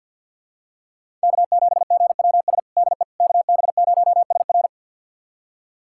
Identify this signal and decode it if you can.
Morse code, heard as a keyed tone, about 50 wpm, 700 Hz. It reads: K7GWS DE KB0SR
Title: USA Callsign DE Callsign - 5 Characters - 50wpm